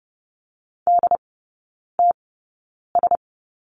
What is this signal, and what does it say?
Morse code, heard as a keyed tone, about 30 wpm, spaced as for 9 wpm, 700 Hz. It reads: DTS
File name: Lesson 16e - Letter c - Three Characters 30wpm